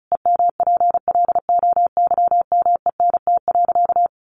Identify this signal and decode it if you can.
Morse code, heard as a keyed tone, about 34 wpm, 700 Hz. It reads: EMPLOYMENT.
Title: Sentences from Top 1500 Words 35wpm